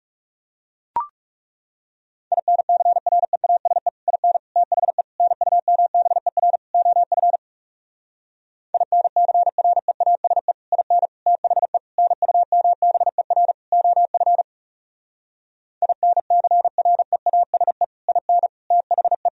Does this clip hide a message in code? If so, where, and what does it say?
Yes: Morse code from 2.3 to 19.3 s, heard as a keyed tone, about 40 wpm, 700 Hz. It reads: INCREASE IN THE NUMBER OF INCREASE IN THE NUMBER OF INCREASE IN THE